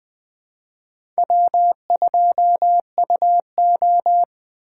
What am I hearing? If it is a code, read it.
Morse code, heard as a keyed tone, about 20 wpm, 700 Hz. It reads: W2UO
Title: USA Call Signs - 4 Characters - 3x Character Spacing - 20wpm